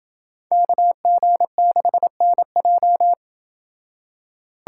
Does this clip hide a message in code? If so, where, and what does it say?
Yes: Morse code from 0.5 to 3.1 s, heard as a keyed tone, about 27 wpm, 700 Hz. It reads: KG6NJ